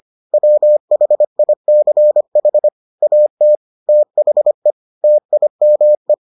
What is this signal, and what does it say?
Morse code, heard as a keyed tone, about 25 wpm, 600 Hz. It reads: WHICH AT THE TIME